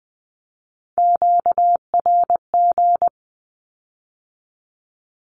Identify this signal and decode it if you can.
Morse code, heard as a keyed tone, about 20 wpm, 700 Hz. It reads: QRG
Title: Q-Codes with Spoken Meaning - 20wpm